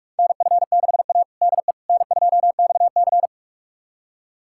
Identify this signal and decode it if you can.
Morse code, heard as a keyed tone, about 47 wpm, 700 Hz. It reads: NF6A DE N2XC
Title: USA Callsign DE Callsign - 4 Characters - 45wpm